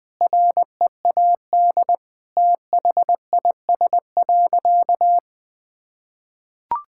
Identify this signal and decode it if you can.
Morse code, heard as a keyed tone, about 20 wpm, 700 Hz. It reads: READ THIS.
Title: Sentences with 8-Word Phrases - 20wpm